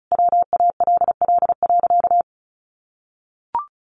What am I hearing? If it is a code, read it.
Morse code, heard as a keyed tone, about 34 wpm, 700 Hz. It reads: WALL.
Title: Sentences from Top 1100 Words 35wpm